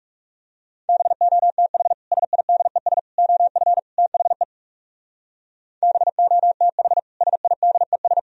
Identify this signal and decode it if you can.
Morse code, heard as a keyed tone, about 35 wpm, 700 Hz. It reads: BOTH SIDES OF THE BOTH SIDES
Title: Speed-Racing - Sets of 4 Words 30wpm